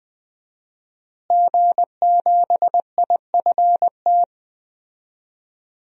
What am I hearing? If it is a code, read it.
Morse code, heard as a keyed tone, about 20 wpm, 700 Hz. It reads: G7IFT
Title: DXCC-100 Call Signs - 5 Characters - Repeated 3x - 20wpm